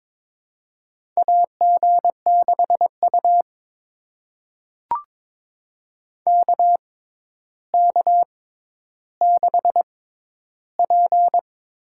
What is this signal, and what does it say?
Morse code, heard as a keyed tone, about 22 wpm, spaced as for 11 wpm, 700 Hz. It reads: AG6U KK6P
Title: USA Call Signs - 4 Characters - 6x Character Spacing - 22wpm